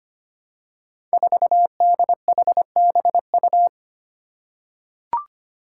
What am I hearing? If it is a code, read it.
Morse code, heard as a keyed tone, about 25 wpm, 700 Hz. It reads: 4DHBU
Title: Five Challenging Letter-Number Combinations 25wpm